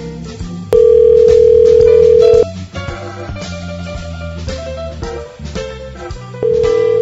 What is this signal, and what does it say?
Ringback tone (North America), 440 + 480 Hz.